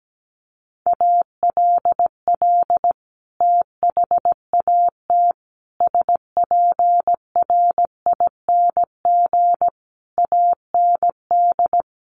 Morse code, heard as a keyed tone, about 17 wpm, 700 Hz. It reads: ALL THAT SPRING AND
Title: Sentences from Top 1200 Words 17wpm